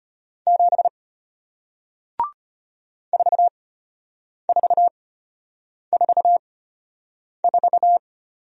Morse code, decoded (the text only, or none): 7 4444